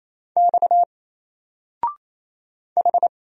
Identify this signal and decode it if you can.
Morse code, heard as a keyed tone, about 27 wpm, 700 Hz. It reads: X H
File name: Single-Letter-Number-28wpm